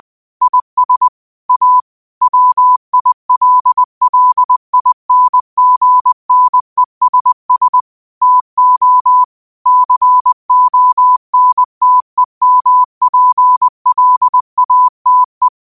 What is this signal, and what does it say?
Morse code, heard as a keyed tone, about 20 wpm, 1000 Hz. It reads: IS A WILLINGNESS TO CONTEMPLATE
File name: qotdfiles / 20wpm